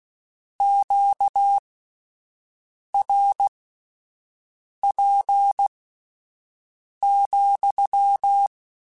Morse code, heard as a keyed tone, about 16 wpm, spaced as for 5 wpm, 800 Hz. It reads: QRP,